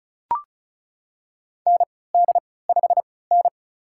Morse code, heard as a keyed tone, about 36 wpm, spaced as for 20 wpm, 700 Hz. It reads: ND5N
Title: USA Call Signs - 4 Characters - 3x Character Spacing - 35wpm